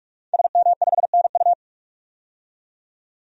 Morse code, heard as a keyed tone, about 44 wpm, 700 Hz. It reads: SM5NU
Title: DXCC-100 Call Signs - 5 Characters - 45wpm